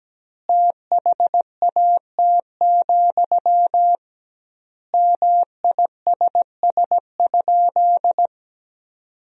Morse code, heard as a keyed tone, about 17 wpm, 700 Hz. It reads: THAT, MISS?
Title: 2x Word-Spacing - Sentences from Top 500 Words - 17wpm